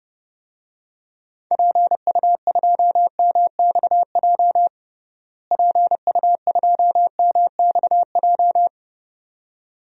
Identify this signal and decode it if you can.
Morse code, heard as a keyed tone, about 31 wpm, 700 Hz. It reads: PU2MXJ PU2MXJ
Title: DXCC-100 Call Signs - 6 Characters - Repeated 2x - 30wpm